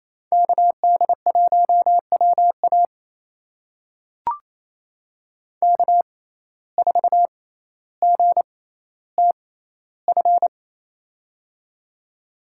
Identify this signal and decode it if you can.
Morse code, heard as a keyed tone, about 28 wpm, spaced as for 14 wpm, 700 Hz. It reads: KD1WA K4GTF
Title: USA Call Signs - 5 Characters - 6x Character Spacing - 28wpm